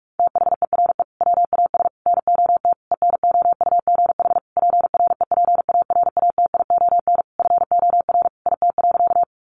Morse code, heard as a keyed tone, about 44 wpm, 700 Hz. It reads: THERE WAS NOT ENOUGH PREPARATION FOR IT.
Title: Sentences from Top 1400 Words 45wpm